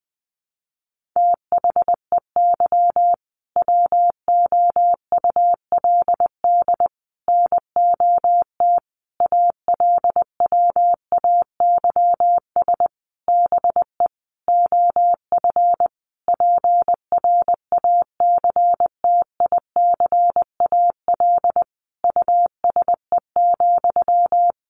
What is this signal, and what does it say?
Morse code, heard as a keyed tone, about 20 wpm, 700 Hz. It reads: THEY WOULD NOT ALWAYS BE OF PRACTICAL USE,
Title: Sentences from Top 1400 Words 20wpm